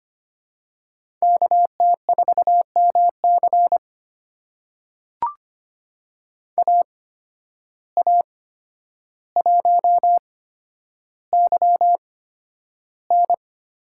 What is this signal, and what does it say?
Morse code, heard as a keyed tone, about 25 wpm, spaced as for 11 wpm, 700 Hz. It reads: KT4MC AA1YN
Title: USA Call Signs - 5 Characters - 8x Character Spacing - 25wpm